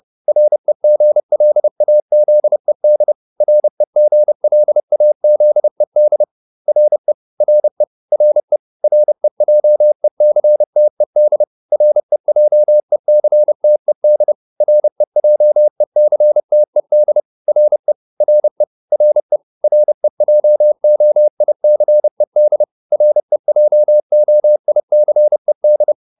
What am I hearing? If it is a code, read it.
Morse code, heard as a keyed tone, about 30 wpm, 600 Hz. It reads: REGLAZED REGLAZED RE RE RE REJECTED REJECTED REJECTED RE RE RE REJOICED REJOICED